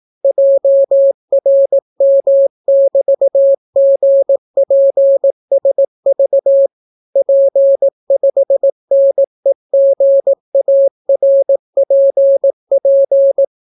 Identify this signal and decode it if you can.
Morse code, heard as a keyed tone, about 18 wpm, 550 Hz. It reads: JRM<BT>GPSV P5NEGARPP